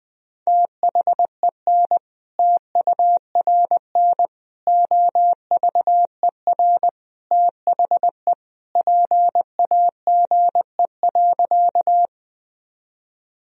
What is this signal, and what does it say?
Morse code, heard as a keyed tone, about 20 wpm, 700 Hz. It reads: THEN TURN OVER THE PAGE.
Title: Sentences from Top 1000 Words - Review Part 1 - 20wpm